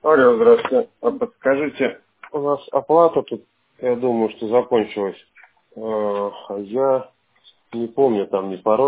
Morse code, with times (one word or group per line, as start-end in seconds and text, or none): none